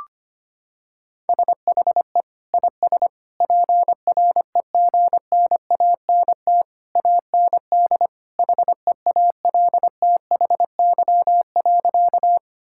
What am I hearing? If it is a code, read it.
Morse code, heard as a keyed tone, about 25 wpm, 700 Hz. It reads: SHE IS PREGNANT AND HEALTHY.
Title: Sentences from Top 1000 Words - Review Part 6 - 25wpm